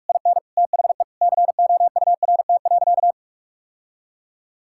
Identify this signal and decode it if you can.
Morse code, heard as a keyed tone, about 46 wpm, 700 Hz. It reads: IN THE COURT.
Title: Sentences from Top 1600 Words 45wpm